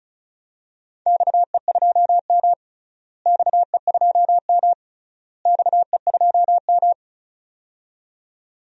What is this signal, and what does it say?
Morse code, heard as a keyed tone, about 36 wpm, 700 Hz. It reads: XE2M XE2M XE2M
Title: DXCC-100 Call Signs - 4 Characters - Repeated 3x - 35wpm